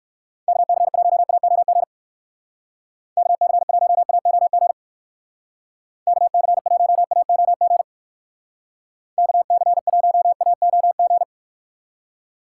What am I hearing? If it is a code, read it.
Morse code, heard as a keyed tone, about 56 wpm, 700 Hz. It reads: KC1AOG KC1AOG KC1AOG KC1AOG
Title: Speed Racing - USA 2x3 Call Signs 45wpm